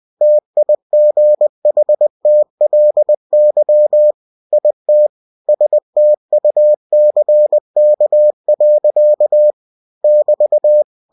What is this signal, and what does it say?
Morse code, heard as a keyed tone, about 20 wpm, 600 Hz. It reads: TIGHTLY IT STUCK. <BT>
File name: cw-books / winnie-the-pooh / winnie-20wpm-page-0042